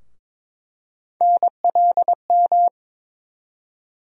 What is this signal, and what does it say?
Morse code, heard as a keyed tone, about 22 wpm, 700 Hz. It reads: NLM
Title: USA Call Sign Postfixes - 3 Letters 22wpm